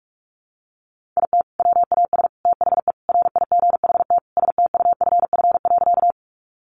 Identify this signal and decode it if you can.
Morse code, heard as a keyed tone, about 44 wpm, 700 Hz. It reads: IT WAS THE RIGHT STUFF.
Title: Sentences from Top 1400 Words 45wpm